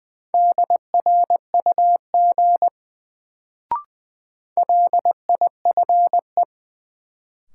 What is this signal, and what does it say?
Morse code, heard as a keyed tone, about 20 wpm, 700 Hz. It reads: DRUG LIFE
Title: Sets of 5 Words - New Word Drill - 20wpm